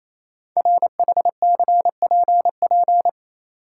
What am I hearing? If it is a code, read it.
Morse code, heard as a keyed tone, about 28 wpm, 700 Hz. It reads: RHCPP
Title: Code Groups - 5 Letters - 28wpm